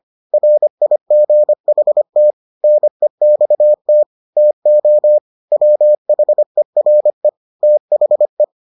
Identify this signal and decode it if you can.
Morse code, heard as a keyed tone, about 25 wpm, 600 Hz. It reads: RIGHT NEXT TO WHERE THE